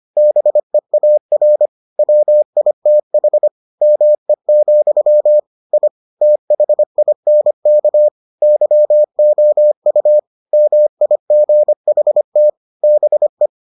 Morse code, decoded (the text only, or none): BEAR WITH ME, I THINK YOU MIGHT BE